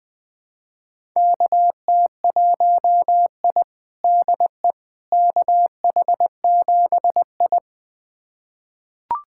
Morse code, heard as a keyed tone, about 21 wpm, 700 Hz. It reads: KT1I DE KH7I
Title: USA Callsign DE Callsign - 4 Characters - 20wpm